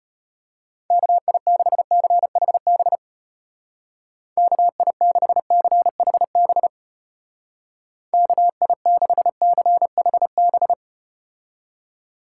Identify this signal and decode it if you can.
Morse code, heard as a keyed tone, about 33 wpm, 700 Hz. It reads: KI6CHB KI6CHB KI6CHB